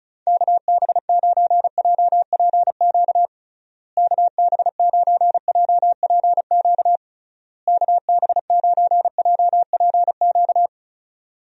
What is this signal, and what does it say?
Morse code, heard as a keyed tone, about 34 wpm, 700 Hz. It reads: KB9JPQ KB9JPQ KB9JPQ
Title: USA Call Signs - 6 Characters - Repeated 3x - 35wpm